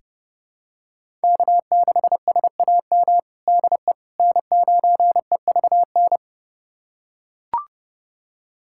Morse code, heard as a keyed tone, about 30 wpm, 700 Hz. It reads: K6SAM DE N9EVN
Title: USA Callsign DE Callsign - 5 Characters - 30wpm